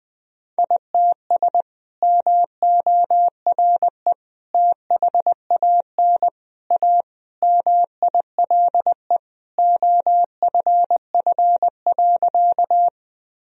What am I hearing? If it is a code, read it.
Morse code, heard as a keyed tone, about 20 wpm, 700 Hz. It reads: ITS MORE THAN A MILE OFF.
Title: Sentences from Top 1000 Words - Review Part 1 - 20wpm